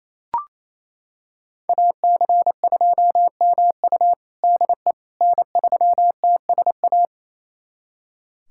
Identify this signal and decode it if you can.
Morse code, heard as a keyed tone, about 28 wpm, 700 Hz. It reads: AC2MU DE N3TSA
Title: USA Callsign DE Callsign - 5 Characters - 28wpm